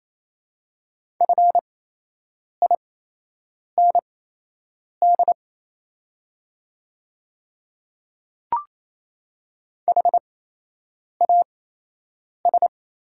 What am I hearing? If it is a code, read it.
Morse code, heard as a keyed tone, about 28 wpm, spaced as for 8 wpm, 700 Hz. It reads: FIND HAS
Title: 8x Character Spacing - Top 200 Words - 28wpm